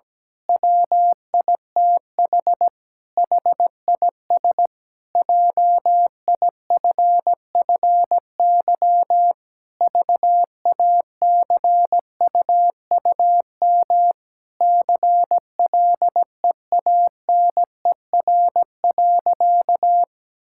Morse code, heard as a keyed tone, about 17 wpm, 700 Hz. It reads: WITH HIS JIFFY VACUUM CLEANER.
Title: Pangram Challenge v2 - 17wpm